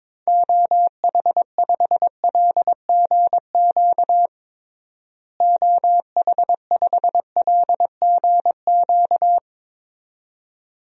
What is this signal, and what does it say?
Morse code, heard as a keyed tone, about 22 wpm, 700 Hz. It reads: OH5LGQ OH5LGQ
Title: DXCC-100 Call Signs - 6 Characters - Repeated 2x - 22wpm